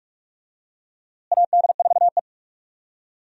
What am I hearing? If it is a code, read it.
Morse code, heard as a keyed tone, about 42 wpm, 700 Hz. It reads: AD4E
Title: USA Call Signs - 4 Characters - Repeated 2x - 45wpm